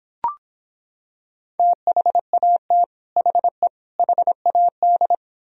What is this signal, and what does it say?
Morse code, heard as a keyed tone, about 26 wpm, 700 Hz. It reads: THAT HE HAD